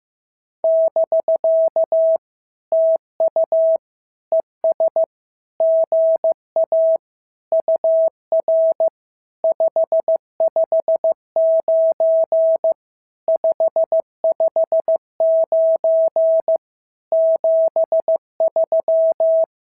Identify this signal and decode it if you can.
Morse code, heard as a keyed tone, about 15 wpm, 650 Hz. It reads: <BK> TU ES GA UR 559 559 73
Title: SOTA QSO Practice 15wpm